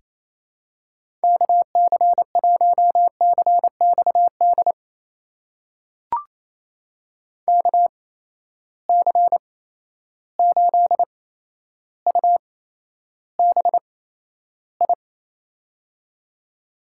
Morse code, heard as a keyed tone, about 28 wpm, spaced as for 12 wpm, 700 Hz. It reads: KC1CXD KC8UBI
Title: USA Call Signs - 6 Characters - 8x Character Spacing - 28wpm